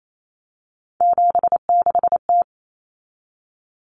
Morse code, heard as a keyed tone, about 28 wpm, 700 Hz. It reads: 76T
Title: Common Cut Numbers - 3-Digit 28wpm